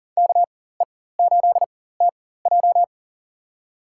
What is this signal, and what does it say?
Morse code, heard as a keyed tone, about 40 wpm, spaced as for 19 wpm, 700 Hz. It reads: KE8TJ
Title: USA Call Signs - 5 Characters - 4x Character Spacing - 40wpm